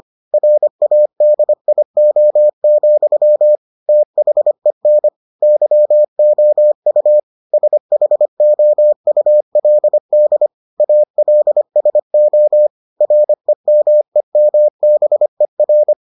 Morse code, heard as a keyed tone, about 25 wpm, 600 Hz. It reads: RADIO, THEN YOU SHOULD ALSO REMEMBER